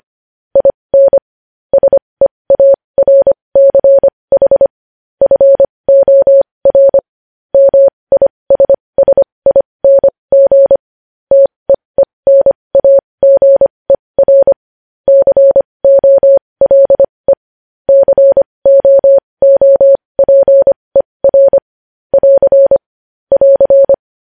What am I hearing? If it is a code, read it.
Morse code, heard as a keyed tone, about 25 wpm, spaced as for 20 wpm, 550 Hz. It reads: IN SEARCH FOR MISSING TEENAGER COLE COOPER <AR> <AR>
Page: News Headlines in Morse Code at 20 WPM